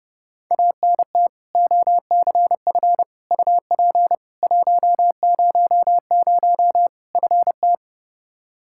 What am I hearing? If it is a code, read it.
Morse code, heard as a keyed tone, about 30 wpm, 700 Hz. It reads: ANT OCF UP 100 FT
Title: Practice QSO - Exchange 2 - Info - 30wpm